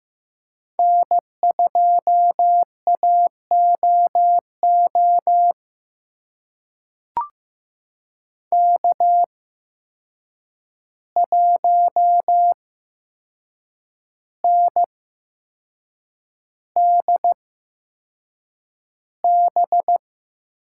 Morse code, heard as a keyed tone, about 15 wpm, spaced as for 6 wpm, 700 Hz. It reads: N2AOO K1NDB
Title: USA Call Signs - 5 Characters - 8x Character Spacing - 15wpm